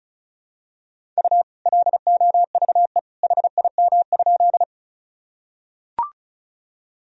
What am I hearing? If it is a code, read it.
Morse code, heard as a keyed tone, about 36 wpm, 700 Hz. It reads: U LOVE HIM?
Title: Sentences from Top 400 Words and Abbreviations 35wpm